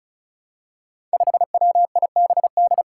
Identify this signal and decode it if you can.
Morse code, heard as a keyed tone, about 36 wpm, 700 Hz. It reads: 5WIBD
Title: Five Challenging Letter-Number Combinations 35wpm